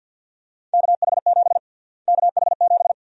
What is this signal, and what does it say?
Morse code, heard as a keyed tone, about 53 wpm, 700 Hz. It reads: KH7 KH7
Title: USA Call Sign Prefixes - Repeated 2x - 50wpm